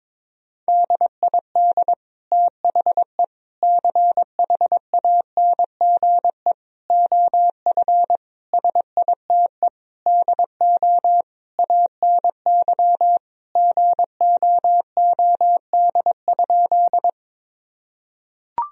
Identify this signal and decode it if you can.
Morse code, heard as a keyed tone, about 21 wpm, 700 Hz. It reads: DID THE CHANGE OF SITE DO ANY GOOD?
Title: Sentences from Top 1000 Words - Review Part 4 - 22wpm